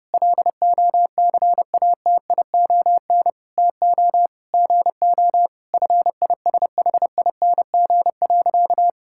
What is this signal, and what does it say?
Morse code, heard as a keyed tone, about 30 wpm, 700 Hz. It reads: LOCATION TO GO FISHING.
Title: Sentences from Top 1000 Words - Review Part 6 - 30wpm